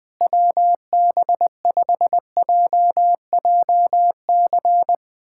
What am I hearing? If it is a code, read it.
Morse code, heard as a keyed tone, about 20 wpm, 700 Hz. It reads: WB5JJC
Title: USA Callsign DE Callsign 20wpm